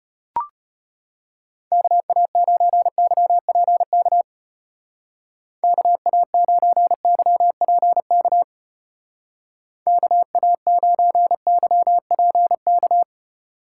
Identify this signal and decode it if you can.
Morse code, heard as a keyed tone, about 31 wpm, 700 Hz. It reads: KA9YPK KA9YPK KA9YPK
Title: Speed Racing - USA 2x3 Call Signs 25wpm